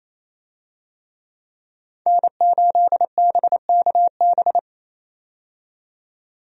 Morse code, heard as a keyed tone, about 28 wpm, 700 Hz. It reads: N8BKB
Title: USA Call Signs - 5 Characters - 28wpm